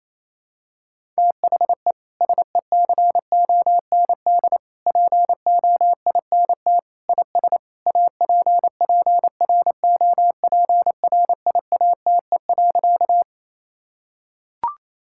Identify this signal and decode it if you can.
Morse code, heard as a keyed tone, about 27 wpm, 700 Hz. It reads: THE SECOND POINT IS APPROPRIATE.
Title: Sentences from Top 1000 Words - Review Part 5 - 28wpm